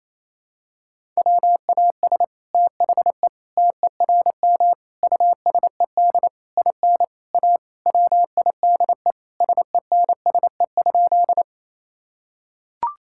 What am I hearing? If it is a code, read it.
Morse code, heard as a keyed tone, about 28 wpm, 700 Hz. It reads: WAS THE TERM USED IN A WIDE SENSE?
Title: Sentences from Top 1000 Words - Review Part 6 - 28wpm